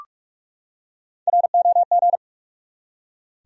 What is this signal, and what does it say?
Morse code, heard as a keyed tone, about 41 wpm, 700 Hz. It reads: ROG
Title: USA Common Names - Max 3 Letters - 45wpm